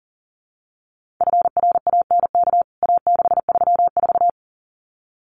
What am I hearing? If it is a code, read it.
Morse code, heard as a keyed tone, about 40 wpm, 700 Hz. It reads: FRANK A634
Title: CWT Exchange 40wpm